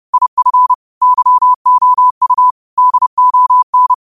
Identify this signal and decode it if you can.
Morse code, heard as a keyed tone, about 30 wpm, 1000 Hz. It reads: IF YOU DON